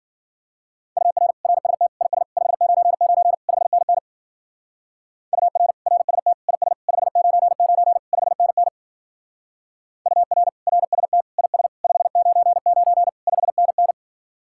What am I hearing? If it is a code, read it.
Morse code, heard as a keyed tone, about 52 wpm, 700 Hz. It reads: UR RST IS 599 5NN UR RST IS 599 5NN UR RST IS 599 5NN